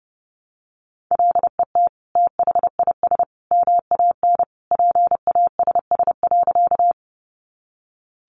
Morse code, heard as a keyed tone, about 30 wpm, 700 Hz. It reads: LET THIS MAN PASS.
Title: Sentences from Top 400 Words 30wpm